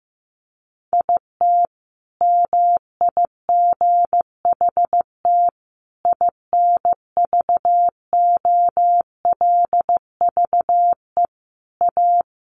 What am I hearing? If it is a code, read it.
Morse code, heard as a keyed tone, about 15 wpm, 700 Hz. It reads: IT MIGHT INVOLVE A
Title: Sentences from Top 1200 Words 15wpm